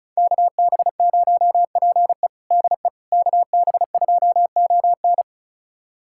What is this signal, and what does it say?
Morse code, heard as a keyed tone, about 36 wpm, 700 Hz. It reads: KB0PE DE KB2ON
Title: USA Callsign DE Callsign - 5 Characters - 35wpm